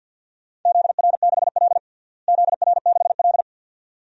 Morse code, heard as a keyed tone, about 53 wpm, 700 Hz. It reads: ZR6L ZR6L
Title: DXCC-100 Call Signs - 4 Characters - Repeated 2x - 50wpm